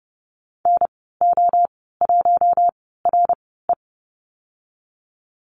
Morse code, heard as a keyed tone, about 30 wpm, spaced as for 17 wpm, 700 Hz. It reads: NO1RE
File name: Lesson 19g - Letter w - Call Signs 30wpm